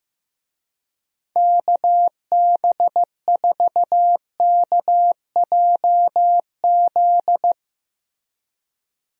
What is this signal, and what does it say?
Morse code, heard as a keyed tone, about 15 wpm, 700 Hz. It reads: KB4KJZ
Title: USA Call Signs - 6 Characters - Repeated 3x - 15wpm